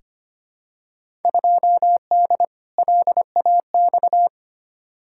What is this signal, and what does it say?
Morse code, heard as a keyed tone, about 25 wpm, 700 Hz. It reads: 2D LAX
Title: Field Day Exchanges - 25wpm